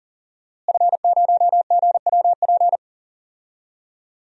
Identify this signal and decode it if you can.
Morse code, heard as a keyed tone, about 37 wpm, 700 Hz. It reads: F0GWP
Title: DXCC-100 Call Signs - 5 Characters - Repeated 2x - 40wpm